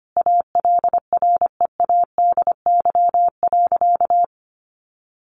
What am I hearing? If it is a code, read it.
Morse code, heard as a keyed tone, about 25 wpm, 700 Hz. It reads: ALREADY.
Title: Sentences from Top 900 Words 25wpm